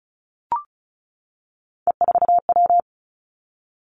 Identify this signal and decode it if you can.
Morse code, heard as a keyed tone, about 36 wpm, 700 Hz. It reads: E4W
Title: Three Challenging Letter-Number Combinations 35wpm